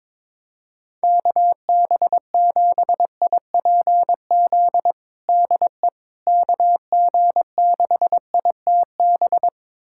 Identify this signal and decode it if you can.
Morse code, heard as a keyed tone, about 22 wpm, 700 Hz. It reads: KB7IPZ DE KG6ITB
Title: USA Callsign DE Callsign 22wpm